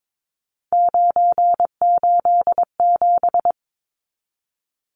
Morse code, heard as a keyed tone, about 22 wpm, 700 Hz. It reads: 987
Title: Numbers - 3-Digit 22wpm